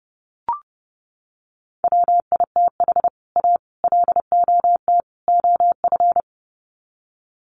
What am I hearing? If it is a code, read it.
Morse code, heard as a keyed tone, about 30 wpm, 700 Hz. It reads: WITH A LOT OF
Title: Sets of 4 Words 30wpm